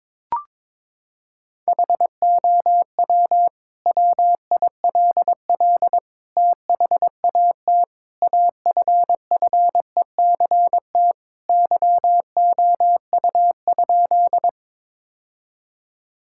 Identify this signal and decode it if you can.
Morse code, heard as a keyed tone, about 22 wpm, 700 Hz. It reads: HOW WILL THAT AFFECT YOU?
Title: Sentences from Top 1000 Words - Review Part 6 - 22wpm